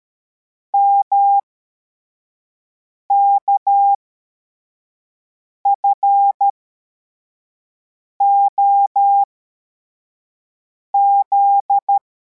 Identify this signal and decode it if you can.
Morse code, heard as a keyed tone, about 13 wpm, spaced as for 4 wpm, 800 Hz. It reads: MKFOZ
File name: formation / cw / Text71